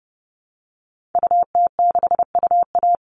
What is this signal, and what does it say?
Morse code, heard as a keyed tone, about 30 wpm, 700 Hz. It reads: UT6UA
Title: Call Signs Repeated 3x - 30wpm